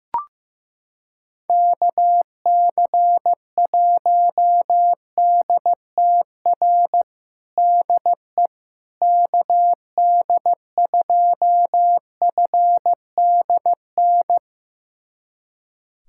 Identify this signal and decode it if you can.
Morse code, heard as a keyed tone, about 15 wpm, 700 Hz. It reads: KC1DTR DE KD2FDN
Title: USA Callsign DE Callsign 15wpm